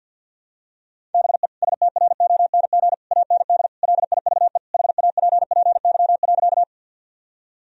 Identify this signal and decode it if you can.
Morse code, heard as a keyed tone, about 50 wpm, 700 Hz. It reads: BE STRONG AND LIVE HAPPY.
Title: Sentences from Top 1000 Words - Review Part 3 - 50wpm